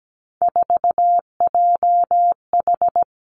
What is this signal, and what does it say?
Morse code, heard as a keyed tone, about 17 wpm, 700 Hz. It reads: 4JH
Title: Three Challenging Letter-Number Combinations 17wpm